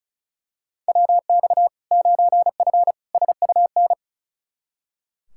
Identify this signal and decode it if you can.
Morse code, heard as a keyed tone, about 36 wpm, 700 Hz. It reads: WX 9F SUN
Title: Practice QSO - Exchange 2 - Info - 35wpm